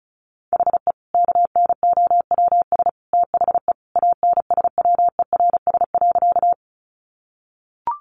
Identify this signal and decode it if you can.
Morse code, heard as a keyed tone, about 34 wpm, 700 Hz. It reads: HE KNOWS THE ANSWERS.
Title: Additional Easy Phrases 35wpm